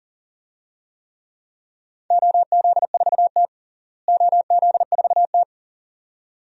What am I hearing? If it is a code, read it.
Morse code, heard as a keyed tone, about 40 wpm, 700 Hz. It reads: OZ4T OZ4T
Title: DXCC-100 Call Signs - 4 Characters - Repeated 2x - 40wpm